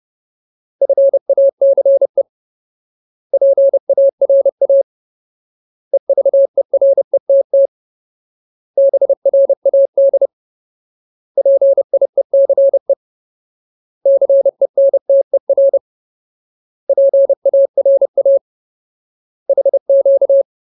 Morse code, heard as a keyed tone, about 30 wpm, 550 Hz. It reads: FACE PARA EVERETT BRAD PIECE CENTER PARA HQ
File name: Ultimate Word List / 30WPM Ultimate